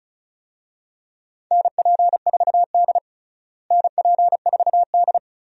Morse code, heard as a keyed tone, about 36 wpm, 700 Hz. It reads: NP4D NP4D
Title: USA Call Signs - 4 Characters - Repeated 3x - 35wpm